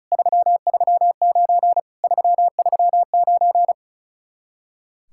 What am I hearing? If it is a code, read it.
Morse code, heard as a keyed tone, about 42 wpm, 700 Hz. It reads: 339 339
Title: Speed-Racing Common QSO Phrases - 35wpm